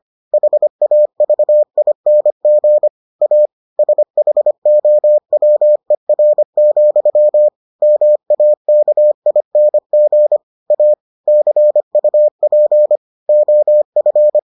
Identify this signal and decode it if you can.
Morse code, heard as a keyed tone, about 25 wpm, 600 Hz. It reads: HAVING A SHOWER, MAKING A CUP OF